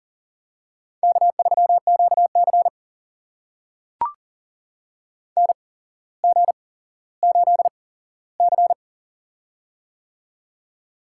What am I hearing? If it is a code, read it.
Morse code, heard as a keyed tone, about 43 wpm, spaced as for 17 wpm, 700 Hz. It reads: K3QC NG8C
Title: USA Call Signs - 4 Characters - 8x Character Spacing - 40wpm